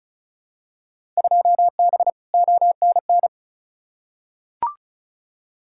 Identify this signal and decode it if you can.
Morse code, heard as a keyed tone, about 36 wpm, 700 Hz. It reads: 2B ONN
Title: Field Day Exchanges - 35wpm